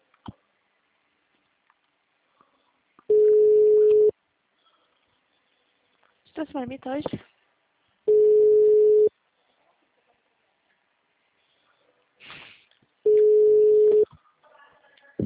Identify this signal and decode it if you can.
Ringback tone (Europe, 425 Hz), with speech.
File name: out- -2003- - - .451